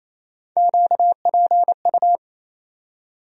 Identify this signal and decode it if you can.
Morse code, heard as a keyed tone, about 28 wpm, 700 Hz. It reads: QPU
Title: USA Call Sign Postfixes - 3 Letters 28wpm